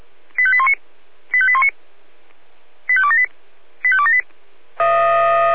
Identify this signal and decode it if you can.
Five-tone selective call (ZVEI), heard as quick runs of five tones, tones of about 70 ms. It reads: 85819 85819 85278 85278